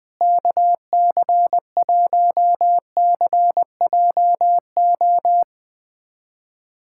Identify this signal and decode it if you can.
Morse code, heard as a keyed tone, about 20 wpm, 700 Hz. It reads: KC1CJO
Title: USA Callsign DE Callsign 20wpm